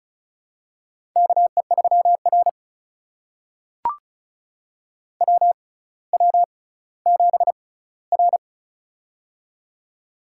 Morse code, heard as a keyed tone, about 36 wpm, spaced as for 18 wpm, 700 Hz. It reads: KE3R WW7R
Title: USA Call Signs - 4 Characters - 6x Character Spacing - 35wpm